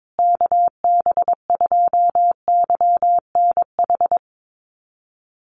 Morse code, heard as a keyed tone, about 22 wpm, 700 Hz. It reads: KB2YNH
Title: USA Call Signs Repeated 3x 22wpm